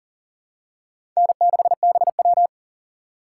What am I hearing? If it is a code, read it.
Morse code, heard as a keyed tone, about 40 wpm, 700 Hz. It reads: N6BW
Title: USA Call Signs - 4 Characters - 4x Character Spacing - 40wpm